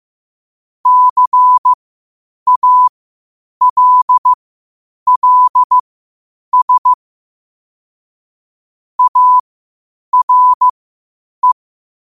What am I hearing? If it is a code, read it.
Morse code, heard as a keyed tone, about 15 wpm, spaced as for 8 wpm, 1000 Hz. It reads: CALLS ARE
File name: qotdfiles / 8wpm